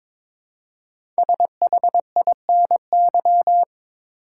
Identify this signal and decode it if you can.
Morse code, heard as a keyed tone, about 22 wpm, 700 Hz. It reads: SHINY
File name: 5 Letter Words - First 3 Letters Given - 22wpm